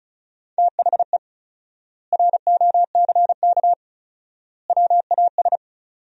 Morse code, heard as a keyed tone, about 34 wpm, 700 Hz. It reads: THE ROCK WAS
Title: 4x Word-Spacing - Sentences from Top 400 Words - 35wpm